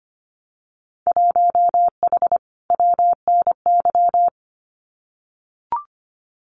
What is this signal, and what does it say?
Morse code, heard as a keyed tone, about 25 wpm, 700 Hz. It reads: 1H WNY
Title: Winter Field Day Exchanges 25wpm